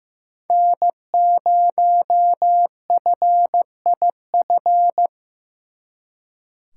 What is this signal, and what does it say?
Morse code, heard as a keyed tone, about 18 wpm, 700 Hz. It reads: N0FIF
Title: Speed Racing - USA 1x3 Call Signs 15wpm